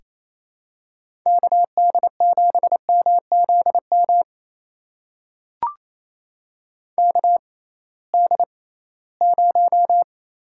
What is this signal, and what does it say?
Morse code, heard as a keyed tone, about 28 wpm, spaced as for 18 wpm, 700 Hz. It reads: KD7MZM KD0
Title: USA Call Signs - 6 Characters - 6x Character Spacing - 28wpm